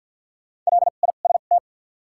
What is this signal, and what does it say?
Morse code, heard as a keyed tone, about 53 wpm, 700 Hz. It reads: L I S T
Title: 2x Character Spacing - Top 500 Words - 50wpm